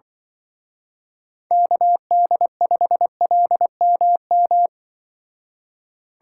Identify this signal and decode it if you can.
Morse code, heard as a keyed tone, about 24 wpm, 700 Hz. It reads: KD5LMM
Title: Speed Racing - USA 2x3 Call Signs 20wpm